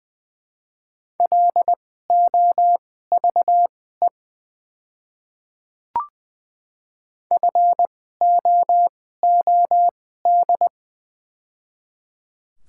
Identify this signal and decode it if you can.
Morse code, heard as a keyed tone, about 20 wpm, spaced as for 14 wpm, 700 Hz. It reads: LOVE FOOD
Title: 2x Character Spacing - Top 500 Words - 20wpm